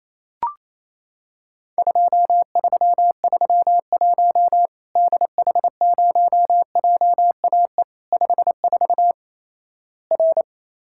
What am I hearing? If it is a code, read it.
Morse code, heard as a keyed tone, about 27 wpm, 700 Hz. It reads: 2331 DH0JAE 54 R
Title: Worked All Europe DX Contest - QTC Exchange - 28wpm